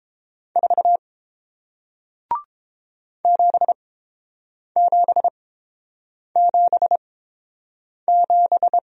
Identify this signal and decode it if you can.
Morse code, heard as a keyed tone, about 30 wpm, spaced as for 7 wpm, 700 Hz. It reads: 4 7777